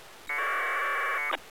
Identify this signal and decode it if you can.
Packet radio (AX.25, as in APRS).